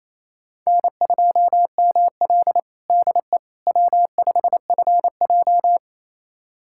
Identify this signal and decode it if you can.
Morse code, heard as a keyed tone, about 28 wpm, 700 Hz. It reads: N2ML DE W5FJ
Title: USA Callsign DE Callsign - 4 Characters - 28wpm